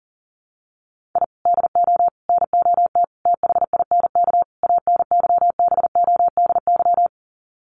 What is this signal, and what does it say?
Morse code, heard as a keyed tone, about 40 wpm, 700 Hz. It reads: I DO NOT THINK ANYBODY
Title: Sets of 5 Words 40wpm